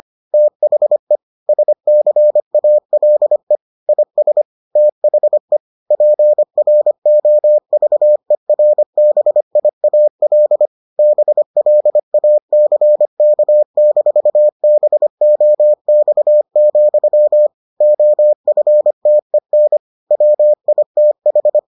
Morse code, heard as a keyed tone, about 25 wpm, 600 Hz. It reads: THE SCALE IS THE PROVERBIAL BLACK-BOX, OFTEN WITH